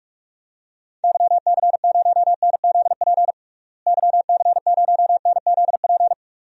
Morse code, heard as a keyed tone, about 46 wpm, 700 Hz. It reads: YC0NZP YC0NZP
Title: DXCC-100 Call Signs - 6 Characters - Repeated 2x - 45wpm